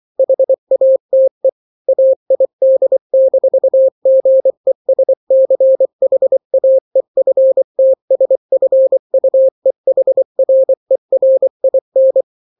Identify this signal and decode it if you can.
Morse code, heard as a keyed tone, about 24 wpm, 530 Hz. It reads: HATE AID-GESCHAEFTSFUEHRERIN
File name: daily / 23wpm